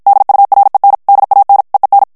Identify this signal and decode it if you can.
Morse code, heard as a keyed tone, about 51 wpm, 800 Hz. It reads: DUREN DAN EEN
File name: HOLLANDAIS texte clair 2 vitesse 50 mots minute